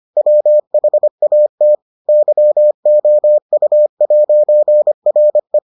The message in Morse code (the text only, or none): WHAT YOU'RE